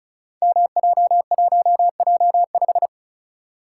Morse code, heard as a keyed tone, about 36 wpm, 700 Hz. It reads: MJ1J5
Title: Code Groups - 5 Letter:Numbers - 35wpm